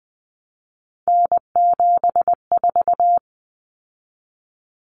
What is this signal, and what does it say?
Morse code, heard as a keyed tone, about 20 wpm, 700 Hz. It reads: N74
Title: Common Cut Numbers - 3-Digit 20wpm